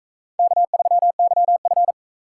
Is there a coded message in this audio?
Yes: Morse code from 0.4 to 1.9 s, heard as a keyed tone, about 39 wpm, 700 Hz. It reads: K3YF